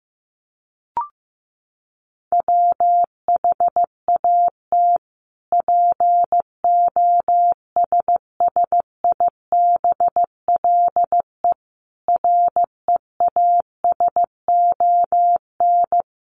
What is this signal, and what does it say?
Morse code, heard as a keyed tone, about 15 wpm, 700 Hz. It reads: WHAT POSSIBLE REASON
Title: Sentences from Top 1300 Words 15wpm